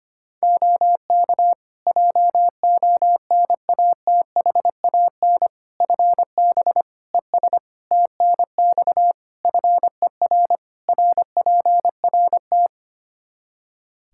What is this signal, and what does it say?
Morse code, heard as a keyed tone, about 25 wpm, 700 Hz. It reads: OK JONATHAN FB ES TNX FER RPRT
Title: Practice QSO - Exchange 2 - Info - 25wpm